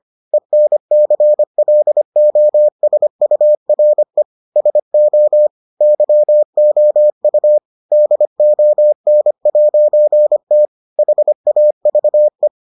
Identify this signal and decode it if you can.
Morse code, heard as a keyed tone, about 25 wpm, 600 Hz. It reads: ENCLOSURE SO YOU DON'T HAVE